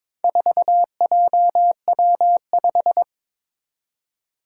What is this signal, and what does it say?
Morse code, heard as a keyed tone, about 22 wpm, 700 Hz. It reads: 4JW5
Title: Four Challenging Letter-Number Combinations 22wpm